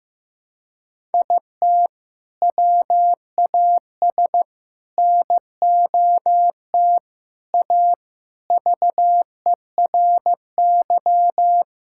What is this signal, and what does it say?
Morse code, heard as a keyed tone, about 15 wpm, 700 Hz. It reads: IT WAS NOT A VERY
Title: Sentences from Top 1000 Words - Review Part 6 - 15wpm